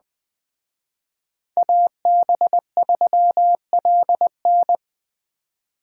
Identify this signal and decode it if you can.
Morse code, heard as a keyed tone, about 20 wpm, 700 Hz. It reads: AB3LN
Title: Speed Racing - USA 2x2 Call Signs 17wpm